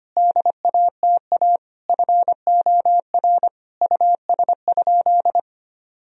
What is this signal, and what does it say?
Morse code, heard as a keyed tone, about 25 wpm, 700 Hz. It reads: DATA FOR US?
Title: Sentences from Top 1000 Words - Review Part 3 - 25wpm